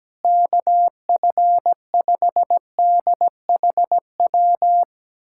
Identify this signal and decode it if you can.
Morse code, heard as a keyed tone, about 17 wpm, 700 Hz. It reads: KF5DHW
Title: USA Callsign DE Callsign - 6 Characters - 17wpm